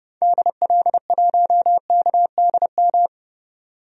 Morse code, heard as a keyed tone, about 30 wpm, 700 Hz. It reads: DL1KDM
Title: DXCC-100 Call Signs - 6 Characters - 30wpm